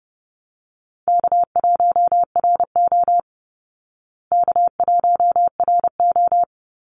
Morse code, heard as a keyed tone, about 30 wpm, 700 Hz. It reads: K1RO K1RO
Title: Easy Call Signs Repeated 3x - 30wpm